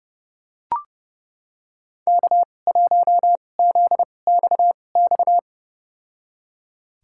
Morse code, heard as a keyed tone, about 30 wpm, spaced as for 22 wpm, 700 Hz. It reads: K1ZXX
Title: USA Call Signs - 5 Characters - 2x Character Spacing - 30wpm